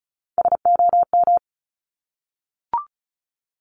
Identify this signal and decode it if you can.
Morse code, heard as a keyed tone, about 36 wpm, 700 Hz. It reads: SOM
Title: Top Three Letter Combinations in Words 35wpm